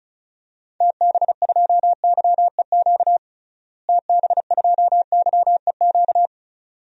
Morse code, heard as a keyed tone, about 34 wpm, 700 Hz. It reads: TB2YEQ TB2YEQ
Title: DXCC-100 Call Signs - 6 Characters - Repeated 2x - 35wpm